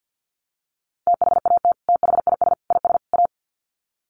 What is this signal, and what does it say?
Morse code, heard as a keyed tone, about 50 wpm, 700 Hz. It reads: THAT THIS IS A
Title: Sets of 4 Words 50wpm